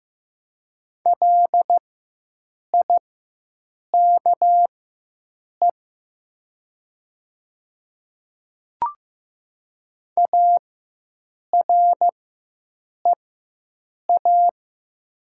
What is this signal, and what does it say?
Morse code, heard as a keyed tone, about 15 wpm, spaced as for 7 wpm, 700 Hz. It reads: LIKE AREA
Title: 4x Character Spacing - Top 500 Words - 15wpm